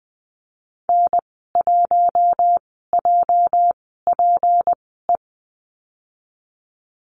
Morse code, heard as a keyed tone, about 20 wpm, spaced as for 15 wpm, 700 Hz. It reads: N1JPE
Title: 238 Lesson 38g - Letter q - Call Signs 20wpm